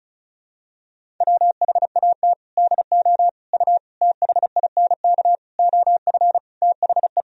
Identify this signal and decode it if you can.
Morse code, heard as a keyed tone, about 36 wpm, 700 Hz. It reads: WHAT DO U THINK OF THE
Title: Sentences from Top 300 Words and Abbreviations 35wpm